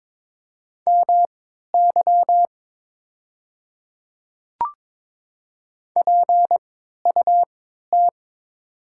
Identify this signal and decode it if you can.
Morse code, heard as a keyed tone, about 22 wpm, spaced as for 12 wpm, 700 Hz. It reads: MY PUT
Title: 3x Character Spacing - Top 300 Words - 22wpm